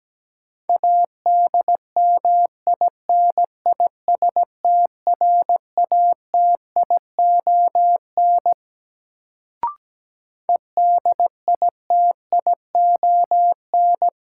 Morse code, heard as a keyed tone, about 17 wpm, 700 Hz. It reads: ADMINISTRATION EDITION
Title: Sets of 5 Words - New Word Drill - 17wpm